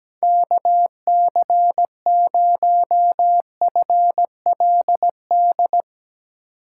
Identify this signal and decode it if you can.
Morse code, heard as a keyed tone, about 19 wpm, 700 Hz. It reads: KC0FLD
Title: Speed Racing - USA 2x3 Call Signs 17wpm